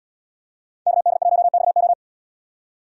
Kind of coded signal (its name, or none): Morse code